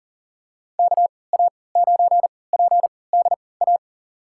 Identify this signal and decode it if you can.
Morse code, heard as a keyed tone, about 40 wpm, spaced as for 23 wpm, 700 Hz. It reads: KA9PDA
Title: USA Call Signs - 6 Characters - 3x Character Spacing - 40wpm